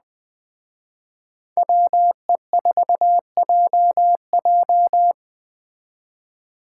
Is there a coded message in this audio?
Yes: Morse code from 1.6 to 5.1 s, heard as a keyed tone, about 22 wpm, 700 Hz. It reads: WE4JJ